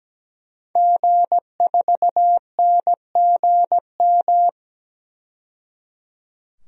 Morse code, heard as a keyed tone, about 17 wpm, 700 Hz. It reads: G4NGM
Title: DXCC-100 Call Signs - 5 Characters - 17wpm